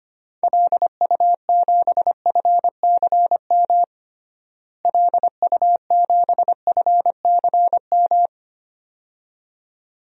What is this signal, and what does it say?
Morse code, heard as a keyed tone, about 25 wpm, 700 Hz. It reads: LU7FCM LU7FCM
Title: DXCC-100 Call Signs - 6 Characters - Repeated 2x - 25wpm